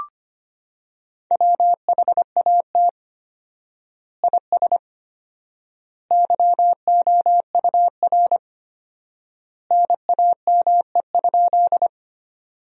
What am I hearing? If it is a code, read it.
Morse code, heard as a keyed tone, about 25 wpm, 700 Hz. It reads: WHAT IS YOUR NAME?
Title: 4x Word-Spacing - Sentences from Top 200 Words - 25wpm